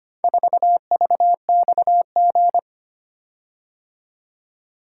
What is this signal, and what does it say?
Morse code, heard as a keyed tone, about 25 wpm, 700 Hz. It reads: 4VXG
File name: practice / 25wpm / Four Letter-Number Combinations - 25wpm